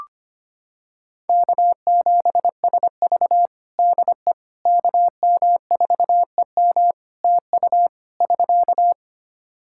Morse code, heard as a keyed tone, about 25 wpm, 700 Hz. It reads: K7SV DE KM4EM TU <SK>
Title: Practice QSO - Exchange 4 - Ending - 25wpm